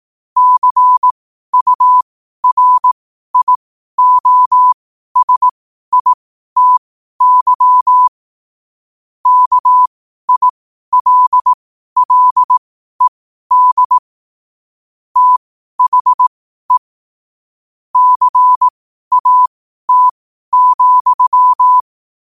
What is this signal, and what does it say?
Morse code, heard as a keyed tone, about 18 wpm, spaced as for 12 wpm, 1000 Hz. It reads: CURIOSITY KILLED THE CAT,
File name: qotdfiles / 12wpm